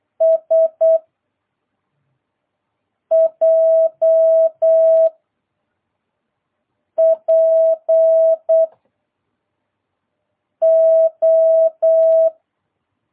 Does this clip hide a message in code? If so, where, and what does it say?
Yes: Morse code from 0.2 to 12.3 s, heard as a keyed tone, about 8 wpm, 660 Hz. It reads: S J P O